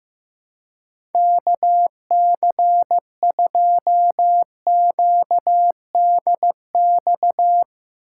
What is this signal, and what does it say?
Morse code, heard as a keyed tone, about 16 wpm, 700 Hz. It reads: KC2QDX
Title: USA Call Signs - 4x Character Spacing - 15wpm